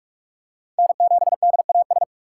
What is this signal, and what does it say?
Morse code, heard as a keyed tone, about 46 wpm, 700 Hz. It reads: N7DAS
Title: USA Call Signs - 2x Character Spacing - 45wpm